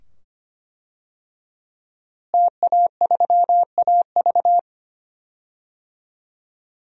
Morse code, heard as a keyed tone, about 25 wpm, 700 Hz. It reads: TA3AV
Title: DXCC-100 Call Signs - 5 Characters - Repeated 2x - 25wpm